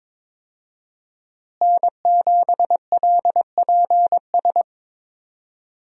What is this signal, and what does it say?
Morse code, heard as a keyed tone, about 22 wpm, 700 Hz. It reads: N7LPS
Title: USA Call Signs - 5 Characters - Repeated 3x - 22wpm